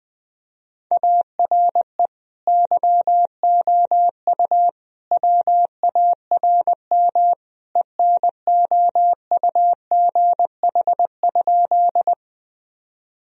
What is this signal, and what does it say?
Morse code, heard as a keyed tone, about 21 wpm, 700 Hz. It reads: ARE YOU WARM ENOUGH?
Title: Sentences from Top 1000 Words - Review Part 2 - 20wpm